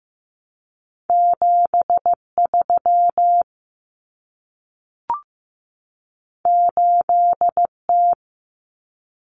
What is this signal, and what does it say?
Morse code, heard as a keyed tone, about 15 wpm, 700 Hz. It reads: 73 8T
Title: Common Cut Numbers - 2-Digit 15wpm